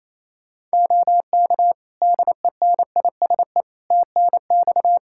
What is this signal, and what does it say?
Morse code, heard as a keyed tone, about 27 wpm, 700 Hz. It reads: OK DENISE TNX
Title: Practice QSO - Exchange 4 - Ending - 28wpm